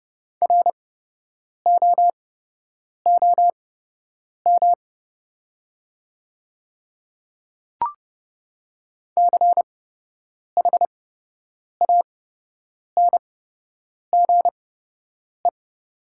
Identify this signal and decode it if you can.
Morse code, heard as a keyed tone, about 30 wpm, spaced as for 8 wpm, 700 Hz. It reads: ROOM CHANGE
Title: 8x Character Spacing - Top 300 Words - 30wpm